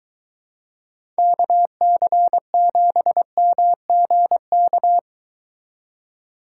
Morse code, heard as a keyed tone, about 22 wpm, 700 Hz. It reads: KC7MGK